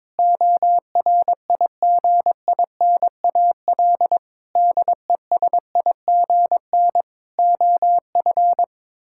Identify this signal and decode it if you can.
Morse code, heard as a keyed tone, about 22 wpm, 700 Hz. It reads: ORIGINAL DESIGN OF
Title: Sentences from Top 1000 Words - Review Part 6 - 22wpm